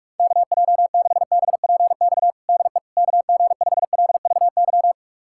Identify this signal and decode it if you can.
Morse code, heard as a keyed tone, about 41 wpm, 700 Hz. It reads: KJ6BPX DE KG5LVY
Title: USA Callsign DE Callsign - 6 Characters - 45wpm